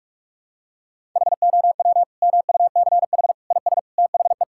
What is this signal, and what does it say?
Morse code, heard as a keyed tone, about 41 wpm, 700 Hz. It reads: HOW MUCH IS THE
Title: Sentences from Top 1000 Words - Review Part 5 - 45wpm